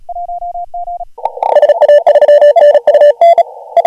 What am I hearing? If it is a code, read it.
Morse code, heard as a keyed tone, about 37 wpm, 590 Hz. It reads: SA3NU
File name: AUDIO2019 / HG1G CW